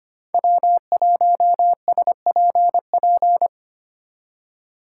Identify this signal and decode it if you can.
Morse code, heard as a keyed tone, about 25 wpm, 700 Hz. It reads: W1SPP
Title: USA Call Signs - 5 Characters - 2x Character Spacing - 25wpm